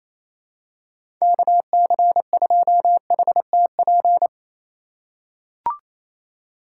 Morse code, heard as a keyed tone, about 28 wpm, 700 Hz. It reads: KC2HTP
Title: USA Call Signs - 6 Characters - 4x Character Spacing - 28wpm